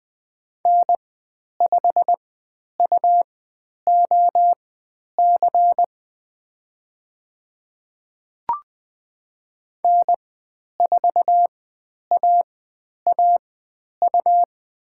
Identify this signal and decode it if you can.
Morse code, heard as a keyed tone, about 20 wpm, spaced as for 10 wpm, 700 Hz. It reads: N5UOC N4AAU
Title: 109 Lesson 18g - Letter m - Call Signs 20wpm